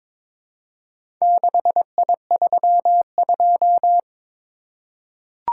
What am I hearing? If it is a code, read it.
Morse code, heard as a keyed tone, about 22 wpm, 700 Hz. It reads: 6I32
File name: practice / 22wpm / Four Letter-Number Combinations - 22wpm